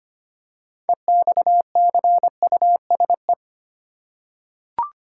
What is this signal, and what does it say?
Morse code, heard as a keyed tone, about 25 wpm, 700 Hz. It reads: EXCUSE
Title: Six Letter Words - 25wpm